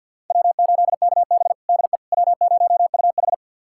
Morse code, heard as a keyed tone, about 52 wpm, 700 Hz. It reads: W8KB DE W0UH
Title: USA Callsign DE Callsign - 4 Characters - 50wpm